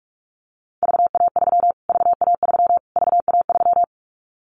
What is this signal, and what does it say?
Morse code, heard as a keyed tone, about 44 wpm, 700 Hz. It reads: VA3 VA3 VA3
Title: Callsign Prefixes for Top 25 DXCC - Repeated 3x - 45wpm